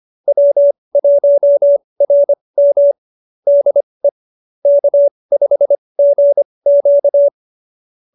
Morse code, heard as a keyed tone, about 25 wpm, spaced as for 20 wpm, 590 Hz. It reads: W1RM DE K5GQ